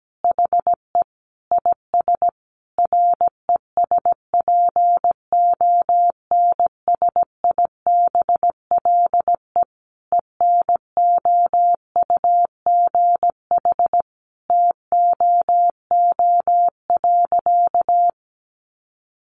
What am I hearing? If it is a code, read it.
Morse code, heard as a keyed tone, about 17 wpm, 700 Hz. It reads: HE IS RESPONSIBLE ENOUGH TOO.
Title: Easy Phrases 17wpm